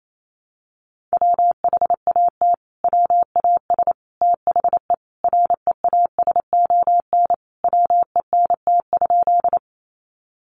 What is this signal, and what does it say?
Morse code, heard as a keyed tone, about 28 wpm, 700 Hz. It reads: WHAT WAS THE REASON WENT?
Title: Sentences from Top 700 Words 28wpm